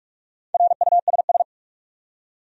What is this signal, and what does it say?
Morse code, heard as a keyed tone, about 41 wpm, 700 Hz. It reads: RUSS
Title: USA Common Names - Max 4 Letters - 45wpm